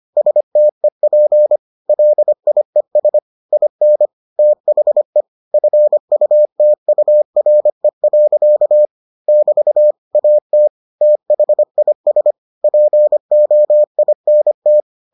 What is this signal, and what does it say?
Morse code, heard as a keyed tone, about 25 wpm, 600 Hz. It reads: STEP LIES IN THE FUTURE. <BT> AT THIS POINT